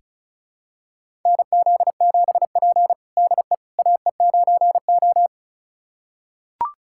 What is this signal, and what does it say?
Morse code, heard as a keyed tone, about 36 wpm, 700 Hz. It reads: NZ7P DE AE9O
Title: USA Callsign DE Callsign - 4 Characters - 35wpm